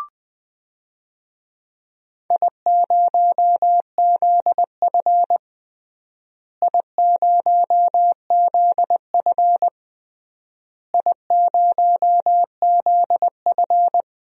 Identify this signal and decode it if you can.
Morse code, heard as a keyed tone, about 20 wpm, 700 Hz. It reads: I0ZF I0ZF I0ZF
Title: DXCC-100 Call Signs - 4 Characters - Repeated 3x - 20wpm